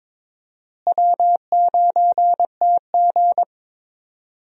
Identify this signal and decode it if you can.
Morse code, heard as a keyed tone, about 22 wpm, 700 Hz. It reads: W9TG
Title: USA Call Signs - 4 Characters - Repeated 2x - 22wpm